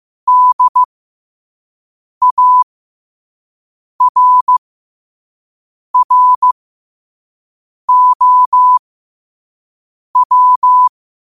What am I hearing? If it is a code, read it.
Morse code, heard as a keyed tone, about 15 wpm, spaced as for 5 wpm, 1000 Hz. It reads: DARROW
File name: qotdfiles / 5wpm